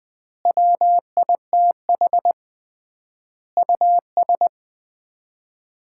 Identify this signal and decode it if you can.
Morse code, heard as a keyed tone, about 20 wpm, 700 Hz. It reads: WITH US
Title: 3x Word-Spacing - Sets of 5 Words - 20wpm